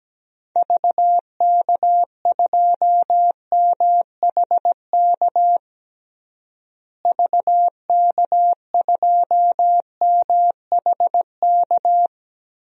Morse code, heard as a keyed tone, about 17 wpm, 700 Hz. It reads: VK2MHK VK2MHK
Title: DXCC-100 Call Signs - 6 Characters - Repeated 2x - 17wpm